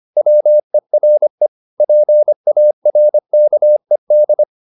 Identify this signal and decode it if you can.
Morse code, heard as a keyed tone, about 25 wpm, 600 Hz. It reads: WERE PARKED